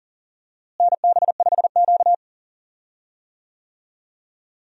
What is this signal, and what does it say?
Morse code, heard as a keyed tone, about 40 wpm, 700 Hz. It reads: NB5Q
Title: USA Call Signs - 4 Characters - Repeated 3x - 40wpm